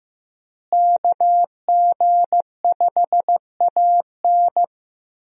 Morse code, heard as a keyed tone, about 15 wpm, 700 Hz. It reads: KG5AN
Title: USA Call Signs - 5 Characters - Repeated 2x - 15wpm